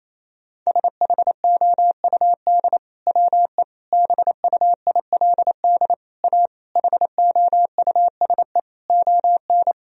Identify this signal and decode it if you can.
Morse code, heard as a keyed tone, about 28 wpm, 700 Hz. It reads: SHOUD WE BUILD A HOUSE ON
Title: Sentences from Top 1000 Words - Review Part 4 - 28wpm